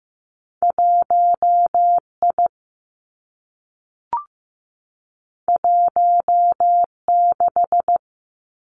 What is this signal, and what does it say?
Morse code, heard as a keyed tone, about 15 wpm, 700 Hz. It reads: 1 I 1 6
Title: Two Challenging Letter-Number Combinations 15wpm